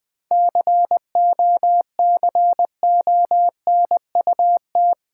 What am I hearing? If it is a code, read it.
Morse code, heard as a keyed tone, about 20 wpm, 690 Hz. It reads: COCONUT
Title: Difficult Words with Hints - 20wpm